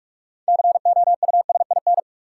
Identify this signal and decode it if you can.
Morse code, heard as a keyed tone, about 44 wpm, 700 Hz. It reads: COUSIN
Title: Six Letter Words - 45wpm